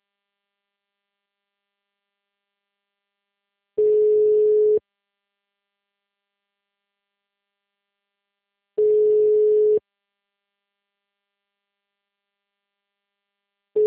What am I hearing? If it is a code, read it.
Ringback tone (Europe, 425 Hz).
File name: out- -2029- - - .16091